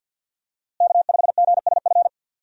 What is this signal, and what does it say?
Morse code, heard as a keyed tone, about 51 wpm, 700 Hz. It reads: K5GSF
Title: USA Call Signs - 5 Characters - Repeated 3x - 50wpm